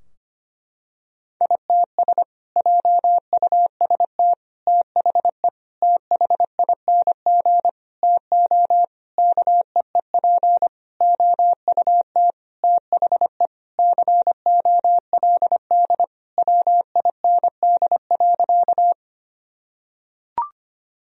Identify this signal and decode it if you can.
Morse code, heard as a keyed tone, about 25 wpm, 700 Hz. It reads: ITS JUST THE THING TO KEEP OUT THE COLD WIND.
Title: Sentences from Top 1000 Words - Review Part 2 - 25wpm